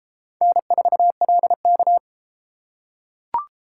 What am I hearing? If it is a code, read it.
Morse code, heard as a keyed tone, about 30 wpm, 700 Hz. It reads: N4LK